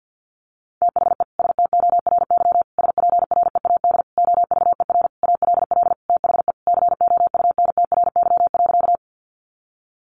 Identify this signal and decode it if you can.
Morse code, heard as a keyed tone, about 50 wpm, 700 Hz. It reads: THE STORY SPREAD OVER ALL THE COUNTRY.
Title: Sentences from Top 1200 Words 50wpm